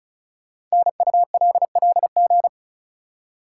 Morse code, heard as a keyed tone, about 36 wpm, 700 Hz. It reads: NULLG
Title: Code Groups - 5 Letter:Numbers - 35wpm